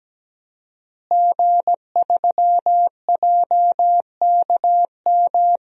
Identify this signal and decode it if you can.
Morse code, heard as a keyed tone, about 17 wpm, 700 Hz. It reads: G3JKM
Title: DXCC-100 Call Signs - 5 Characters - Repeated 2x - 17wpm